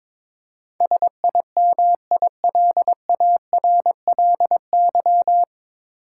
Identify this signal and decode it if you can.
Morse code, heard as a keyed tone, about 22 wpm, 700 Hz. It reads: SIMILARLY
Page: Morse Code Ninja practice 22wpm